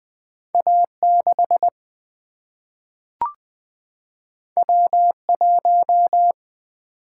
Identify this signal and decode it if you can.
Morse code, heard as a keyed tone, about 20 wpm, 700 Hz. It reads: A 6 W 1
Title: Single-Letter Single-Number 20wpm